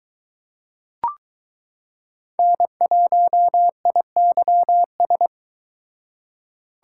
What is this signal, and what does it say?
Morse code, heard as a keyed tone, about 22 wpm, 700 Hz. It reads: N1IYS